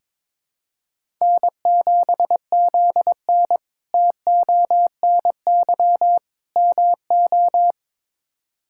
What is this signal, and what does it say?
Morse code, heard as a keyed tone, about 22 wpm, 700 Hz. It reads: N7ZN TONY MO
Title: K1USN SST QSO Practice - 22wpm